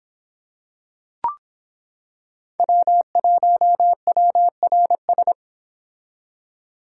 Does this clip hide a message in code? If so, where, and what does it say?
Yes: Morse code from 2.6 to 5.3 s, heard as a keyed tone, about 25 wpm, 700 Hz. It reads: W1WRS